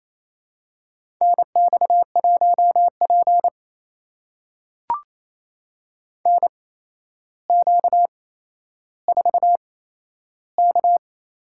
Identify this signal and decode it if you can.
Morse code, heard as a keyed tone, about 27 wpm, spaced as for 12 wpm, 700 Hz. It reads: NX1P NQ4K
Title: USA Call Signs - 4 Characters - 8x Character Spacing - 28wpm